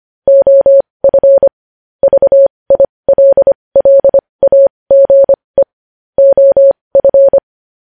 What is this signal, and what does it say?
Morse code, heard as a keyed tone, about 25 wpm, spaced as for 20 wpm, 550 Hz. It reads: OF VILLAGE OF
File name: podcast / News headlines-1x-25x20